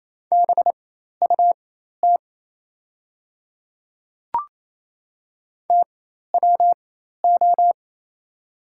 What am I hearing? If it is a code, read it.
Morse code, heard as a keyed tone, about 28 wpm, spaced as for 13 wpm, 700 Hz. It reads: BUT TWO
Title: 4x Character Spacing - Top 200 Words - 28wpm